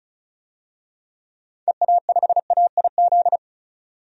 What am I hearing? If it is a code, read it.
Morse code, heard as a keyed tone, about 36 wpm, 700 Hz. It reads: EA5AIZ
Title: DXCC-100 Call Signs - 6 Characters - 35wpm